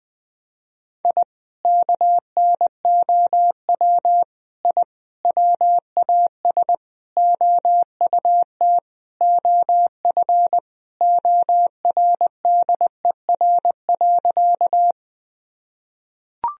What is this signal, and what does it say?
Morse code, heard as a keyed tone, about 20 wpm, 700 Hz. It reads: I KNOW I WAS OUT OF ORDER.
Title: Sentences from Top 1000 Words - Review Part 2 - 20wpm